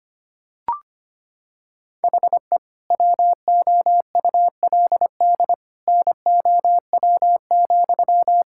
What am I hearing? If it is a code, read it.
Morse code, heard as a keyed tone, about 25 wpm, 700 Hz. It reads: HE WOULD NOW,
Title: Sentences from Top 1000 Words - Review Part 5 - 25wpm